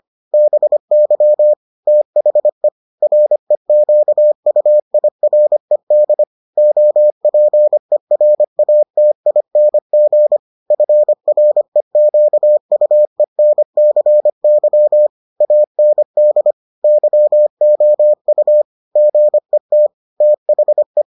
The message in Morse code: BY THE REQUIRED OPERATING FREQUENCY AND YOU GET THE

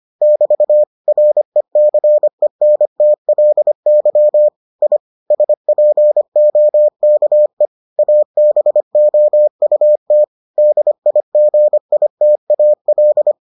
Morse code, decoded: <BT> RECENTLY I SPOKE ABOUT DIGITAL

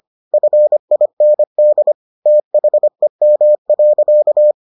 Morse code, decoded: FIND THEM.

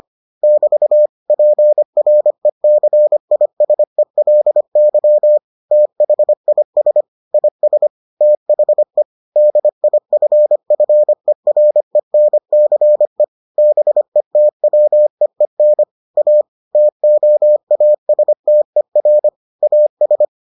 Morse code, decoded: <BT> PRECISELY THIS IS THE DIFFERENCE BETWEEN A TOASTER AS

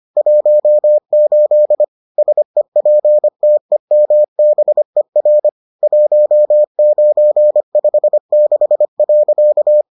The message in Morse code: 18 SEPTEMBER 1956.

